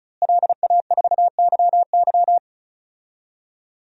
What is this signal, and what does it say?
Morse code, heard as a keyed tone, about 34 wpm, 700 Hz. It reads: LA4YY
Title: DXCC-100 Call Signs - 5 Characters - Repeated 2x - 35wpm